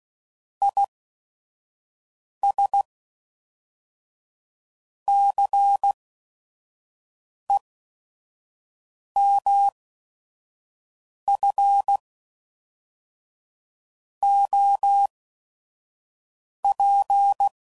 Morse code, heard as a keyed tone, about 16 wpm, 800 Hz. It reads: I S C E M F O P